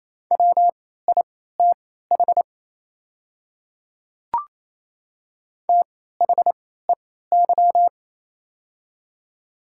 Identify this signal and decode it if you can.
Morse code, heard as a keyed tone, about 28 wpm, spaced as for 16 wpm, 700 Hz. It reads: WITH THEY
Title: 3x Character Spacing - Top 200 Words - 28wpm